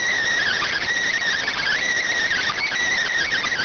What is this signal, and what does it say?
Emergency Alert System (SAME) tones, heard as harsh buzzing bursts. It is heard from 0.4 to 3.0 s.